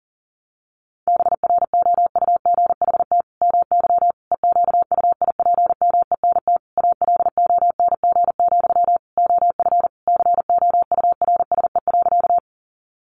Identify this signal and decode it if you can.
Morse code, heard as a keyed tone, about 40 wpm, 700 Hz. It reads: BROUGHT MY EQUIPMENT ALONG, OF COURSE.
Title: Sentences from Top 700 Words 40wpm